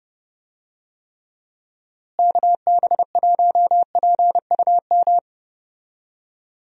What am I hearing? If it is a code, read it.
Morse code, heard as a keyed tone, about 30 wpm, 700 Hz. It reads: KB1PUM
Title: USA Call Signs - 6 Characters - Repeated 2x - 30wpm